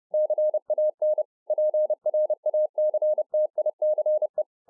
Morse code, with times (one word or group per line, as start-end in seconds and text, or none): 0.1-1.2 CAN
1.5-4.4 PRACTICE